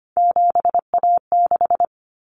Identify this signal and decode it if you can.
Morse code, heard as a keyed tone, about 25 wpm, 700 Hz. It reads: 7A6
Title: Common Cut Numbers - 3-Digit 25wpm